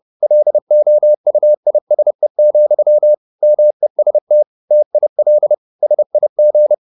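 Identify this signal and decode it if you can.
Morse code, heard as a keyed tone, about 30 wpm, 600 Hz. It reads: LOUISE, MEST TIL SIG
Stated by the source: High-speed Morse (QRQ) at 30 wpm.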